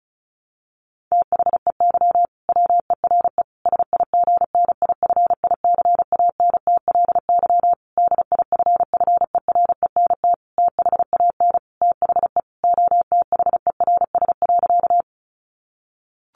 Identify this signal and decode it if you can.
Morse code, heard as a keyed tone, about 34 wpm, 700 Hz. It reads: THEY WERE SIGNIFICANTLY DIFFERENT THAN THE OTHERS.
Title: Sentences from Top 1500 Words 35wpm